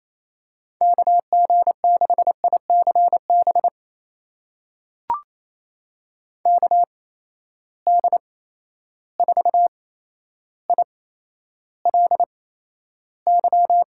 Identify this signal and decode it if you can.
Morse code, heard as a keyed tone, about 27 wpm, spaced as for 12 wpm, 700 Hz. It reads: KG6ICB KD4ILY
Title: USA Call Signs - 6 Characters - 8x Character Spacing - 28wpm